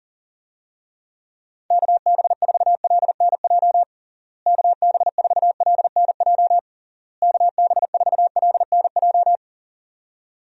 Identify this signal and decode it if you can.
Morse code, heard as a keyed tone, about 40 wpm, 700 Hz. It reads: KB4LNJ KB4LNJ KB4LNJ
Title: USA Call Signs - 6 Characters - Repeated 3x - 40wpm